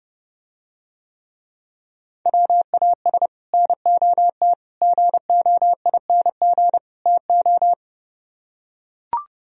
Morse code, heard as a keyed tone, about 32 wpm, 700 Hz. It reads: WAS NOT GOING TO